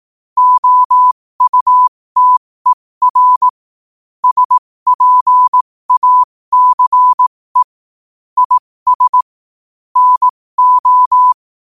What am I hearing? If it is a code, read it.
Morse code, heard as a keyed tone, about 18 wpm, spaced as for 15 wpm, 1000 Hz. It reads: OUTER SPACE IS NO
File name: qotdfiles / 15wpm